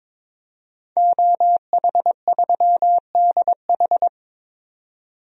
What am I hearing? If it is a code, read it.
Morse code, heard as a keyed tone, about 21 wpm, 700 Hz. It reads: OH3DH
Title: DXCC-100 Call Signs - 5 Characters - 22wpm